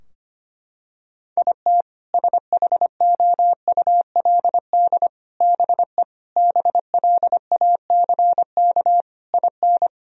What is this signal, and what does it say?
Morse code, heard as a keyed tone, about 25 wpm, 700 Hz. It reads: IT SHOULD BE BLACK IN
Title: Sentences from Top 1000 Words - Review Part 2 - 25wpm